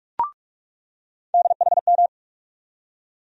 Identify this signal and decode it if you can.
Morse code, heard as a keyed tone, about 46 wpm, 700 Hz. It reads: DHM
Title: USA Call Sign Postfixes - 3 Letters 45wpm